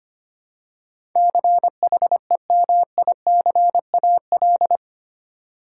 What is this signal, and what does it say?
Morse code, heard as a keyed tone, about 25 wpm, 700 Hz. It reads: CHEMICAL
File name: 8 Letter Words - First 3 Letters Given - 25wpm